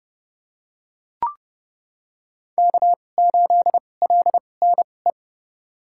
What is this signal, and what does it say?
Morse code, heard as a keyed tone, about 32 wpm, spaced as for 22 wpm, 700 Hz. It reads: K8LNE
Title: USA Call Signs - 2x Character Spacing - 30wpm